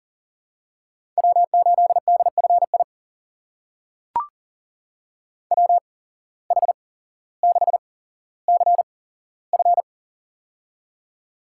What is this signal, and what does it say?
Morse code, heard as a keyed tone, about 40 wpm, spaced as for 17 wpm, 700 Hz. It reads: W8DFI WH6CF
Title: USA Call Signs - 8x Character Spacing - 40wpm